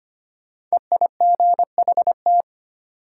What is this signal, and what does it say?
Morse code, heard as a keyed tone, about 25 wpm, 700 Hz. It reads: EIGHT
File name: 5 Letter Words - First 3 Letters Given - 25wpm